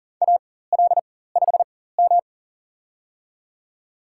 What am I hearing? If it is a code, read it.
Morse code, heard as a keyed tone, about 40 wpm, spaced as for 19 wpm, 700 Hz. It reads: AL5M
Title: USA Call Signs - 4 Characters - 4x Character Spacing - 40wpm